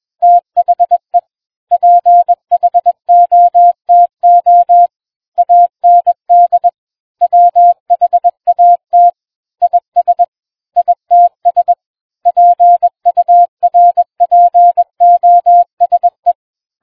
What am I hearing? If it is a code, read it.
Morse code, heard as a keyed tone, about 21 wpm, 700 Hz. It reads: THE PHOTO AND WHAT IS ITS PURPOSE